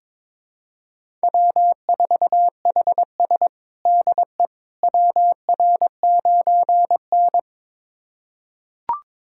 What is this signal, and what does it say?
Morse code, heard as a keyed tone, about 21 wpm, 700 Hz. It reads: W4HS DE WR9N
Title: USA Callsign DE Callsign - 4 Characters - 22wpm